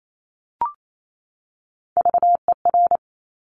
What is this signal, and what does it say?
Morse code, heard as a keyed tone, about 28 wpm, 700 Hz. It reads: VER
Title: Top Three Letter Combinations in Words 28wpm